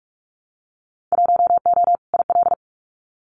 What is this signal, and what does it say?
Morse code, heard as a keyed tone, about 46 wpm, 700 Hz. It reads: 1O IL
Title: Winter Field Day Exchanges 45wpm